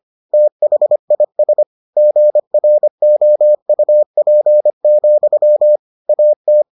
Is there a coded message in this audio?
Yes: Morse code from 0.3 to 6.6 s, heard as a keyed tone, about 25 wpm, 600 Hz. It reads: THIS GROUP, AT